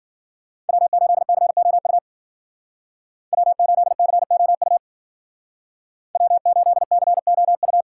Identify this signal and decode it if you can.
Morse code, heard as a keyed tone, about 52 wpm, 700 Hz. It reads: W8COU W8COU W8COU